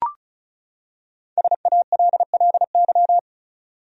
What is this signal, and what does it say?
Morse code, heard as a keyed tone, about 36 wpm, 700 Hz. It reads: SALLY
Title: USA Common Names - Max 5 Letters - 35wpm